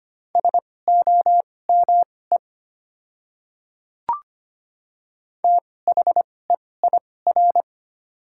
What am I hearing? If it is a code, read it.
Morse code, heard as a keyed tone, about 25 wpm, spaced as for 18 wpm, 700 Hz. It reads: SOME THEIR
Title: 2x Character Spacing - Top 200 Words - 25wpm